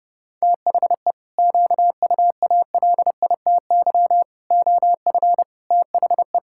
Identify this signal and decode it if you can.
Morse code, heard as a keyed tone, about 30 wpm, 700 Hz. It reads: THE QUALITY OF THE
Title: Sentences from Top 1000 Words - Review Part 6 - 30wpm